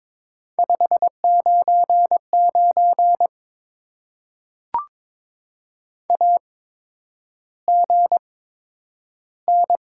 Morse code, heard as a keyed tone, about 22 wpm, spaced as for 9 wpm, 700 Hz. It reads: 599 AGN
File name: Top 90 QSO Elements - 8x Character Spacing - 22wpm